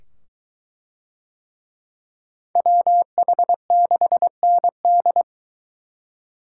Morse code, heard as a keyed tone, about 22 wpm, 700 Hz. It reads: WH6ND